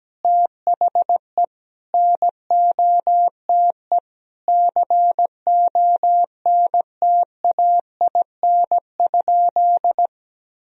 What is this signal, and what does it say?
Morse code, heard as a keyed tone, about 17 wpm, 700 Hz. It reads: THE NOTE CONTAIN?
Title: Sentences from Top 1000 Words - Review Part 2 - 17wpm